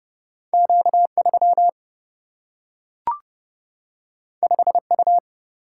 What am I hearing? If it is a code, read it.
Morse code, heard as a keyed tone, about 30 wpm, 700 Hz. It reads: Q3 5U
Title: Single-Letter Single-Number 30wpm